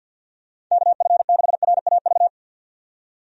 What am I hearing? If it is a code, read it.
Morse code, heard as a keyed tone, about 50 wpm, 700 Hz. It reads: KF6RAV
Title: USA Call Signs - 6 Characters - 50wpm